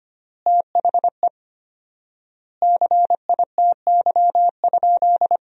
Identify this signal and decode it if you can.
Morse code, heard as a keyed tone, about 25 wpm, 700 Hz. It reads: THE CITY?
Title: 4x Word-Spacing - Sentences from Top 300 Words - 25wpm